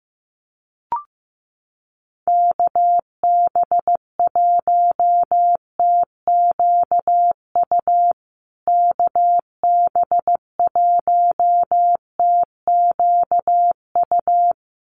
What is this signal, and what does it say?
Morse code, heard as a keyed tone, about 15 wpm, 700 Hz. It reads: KB1TQU KB1TQU
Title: USA Call Signs Repeated 2x 15wpm